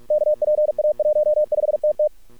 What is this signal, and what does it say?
Morse code, heard as a keyed tone, about 46 wpm, 600 Hz. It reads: KPA15TT
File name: archived / CRSnet